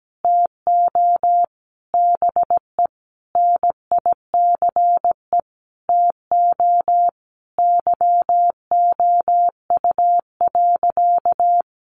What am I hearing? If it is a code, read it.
Morse code, heard as a keyed tone, about 17 wpm, 700 Hz. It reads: TO BE NICE TO YOU.
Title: Sentences from Top 1100 Words 17wpm